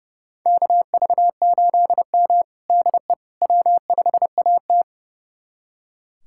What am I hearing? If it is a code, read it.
Morse code, heard as a keyed tone, about 30 wpm, 700 Hz. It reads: KV8M DE W5AT
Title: USA Callsign DE Callsign - 4 Characters - 30wpm